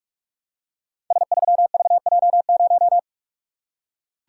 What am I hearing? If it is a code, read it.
Morse code, heard as a keyed tone, about 44 wpm, 700 Hz. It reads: S3VJ0
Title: Code Groups - 5 Letter:Numbers - 45wpm